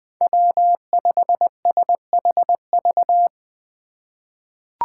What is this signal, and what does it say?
Morse code, heard as a keyed tone, about 20 wpm, 700 Hz. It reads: W5SHV
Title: USA Call Signs - 5 Characters - 4x Character Spacing - 20wpm